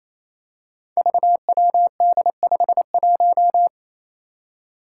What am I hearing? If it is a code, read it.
Morse code, heard as a keyed tone, about 28 wpm, 700 Hz. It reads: VWD51
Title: Five Challenging Letter-Number Combinations 28wpm